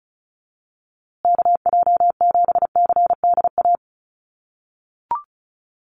Morse code, heard as a keyed tone, about 34 wpm, 700 Hz. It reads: KJ7CDA
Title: USA Call Signs Repeated 2x 35wpm